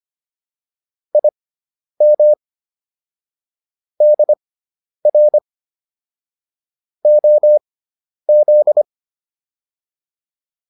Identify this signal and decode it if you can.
Morse code, heard as a keyed tone, about 25 wpm, spaced as for 10 wpm, 600 Hz. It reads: IM DR OZ